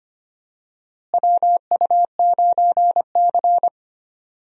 Morse code, heard as a keyed tone, about 25 wpm, 700 Hz. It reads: WU9C
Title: USA Call Signs - 4 Characters - Repeated 2x - 25wpm